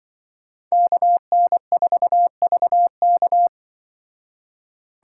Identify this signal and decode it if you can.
Morse code, heard as a keyed tone, about 23 wpm, 700 Hz. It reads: KN4VK
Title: Speed Racing - USA 2x2 Call Signs 20wpm